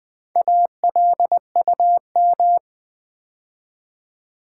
Morse code, heard as a keyed tone, about 20 wpm, 700 Hz. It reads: ALUM
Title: Build A Word - 2 Letters At A Time - 8-Letter Words - 20wpm